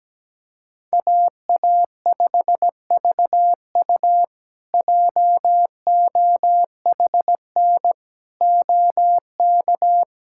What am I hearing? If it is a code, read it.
Morse code, heard as a keyed tone, about 17 wpm, 700 Hz. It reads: AA5VU JOHN OK
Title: K1USN SST QSO Practice - 17wpm